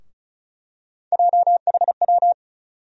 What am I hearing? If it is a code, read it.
Morse code, heard as a keyed tone, about 36 wpm, 700 Hz. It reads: JHW
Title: USA Call Sign Postfixes - 3 Letters 35wpm